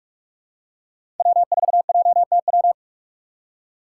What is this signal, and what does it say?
Morse code, heard as a keyed tone, about 46 wpm, 700 Hz. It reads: W4JTW
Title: USA Call Signs - 5 Characters - 4x Character Spacing - 45wpm